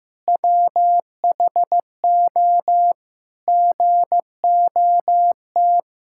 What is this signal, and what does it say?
Morse code, heard as a keyed tone, about 15 wpm, 700 Hz. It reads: WHO GOT
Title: Sentences from Top 1600 Words 15wpm